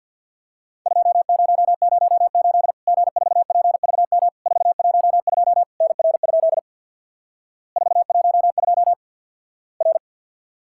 Morse code, heard as a keyed tone, about 50 wpm, 700 Hz. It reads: 2008 G4PVM 412 NR? 412 R
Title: Worked All Europe DX Contest - QTC Exchange with Repeats - 50wpm